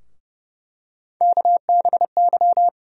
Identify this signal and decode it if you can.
Morse code, heard as a keyed tone, about 30 wpm, 700 Hz. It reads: KBY
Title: USA Call Sign Postfixes - 3 Letters 30wpm